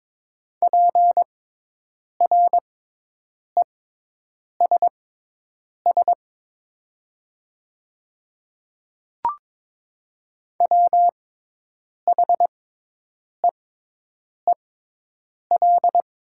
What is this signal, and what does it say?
Morse code, heard as a keyed tone, about 22 wpm, spaced as for 8 wpm, 700 Hz. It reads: PRESS WHEEL
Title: 6x Character Spacing - Top 500 Words - 22wpm